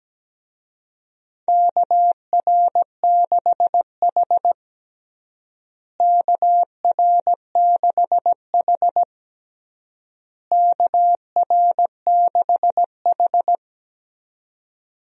Morse code, heard as a keyed tone, about 17 wpm, 700 Hz. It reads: KR6H KR6H KR6H
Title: USA Call Signs - 4 Characters - Repeated 3x - 17wpm